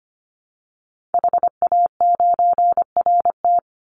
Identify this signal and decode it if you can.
Morse code, heard as a keyed tone, about 25 wpm, 700 Hz. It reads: HA9RT
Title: Call Signs - 25wpm